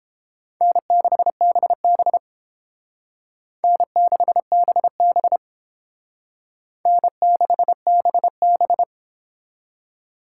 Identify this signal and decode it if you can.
Morse code, heard as a keyed tone, about 31 wpm, 700 Hz. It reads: N6BB N6BB N6BB